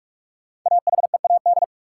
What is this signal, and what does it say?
Morse code, heard as a keyed tone, about 42 wpm, 700 Hz. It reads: AHEAD
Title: Sentences from Top 1100 Words - New Word Drill Alt - 45wpm